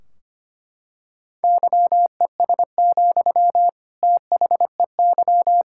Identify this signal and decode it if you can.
Morse code, heard as a keyed tone, about 25 wpm, 700 Hz. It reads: YES, THEY
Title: Sentences from Top 1600 Words 25wpm